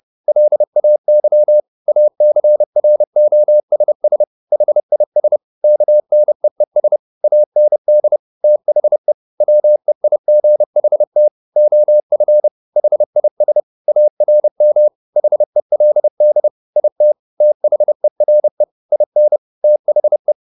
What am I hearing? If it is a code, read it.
Morse code, heard as a keyed tone, about 30 wpm, 600 Hz. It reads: LAY ACROSS HIS KNEES AND THE WEIGHT OF HIS ARM HELD IT THERE IN THE
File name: the old man and the sea / the old man and the sea 1 30wpm